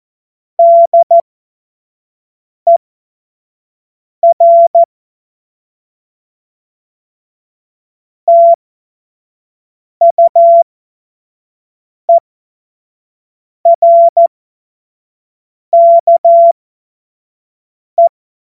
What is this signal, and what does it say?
Morse code, heard as a keyed tone, about 14 wpm, spaced as for 5 wpm, 680 Hz. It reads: DER TUERKE
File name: Witz-14-Oktober-2025-25-F70